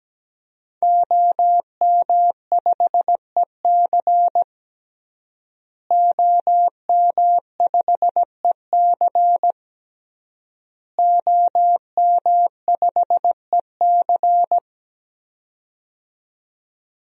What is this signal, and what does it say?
Morse code, heard as a keyed tone, about 17 wpm, 700 Hz. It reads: OM5EC OM5EC OM5EC
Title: DXCC-100 Call Signs - 5 Characters - Repeated 3x - 17wpm